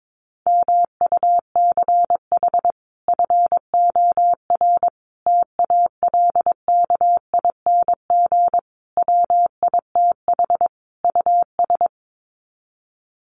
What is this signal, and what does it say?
Morse code, heard as a keyed tone, about 22 wpm, 700 Hz. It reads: MUCH FOR TALKING WITH US
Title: Sets of 5 Words Encore 22wpm